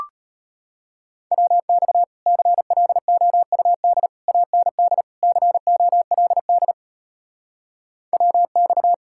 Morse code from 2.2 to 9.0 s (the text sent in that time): CLOUD AND COLD WX